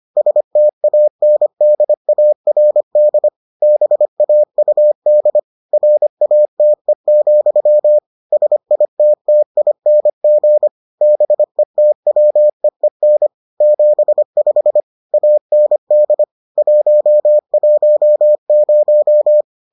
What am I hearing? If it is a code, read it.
Morse code, heard as a keyed tone, about 25 wpm, 600 Hz. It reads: STANDARD BAUD RATE, SITTING BETWEEN 75 AND 110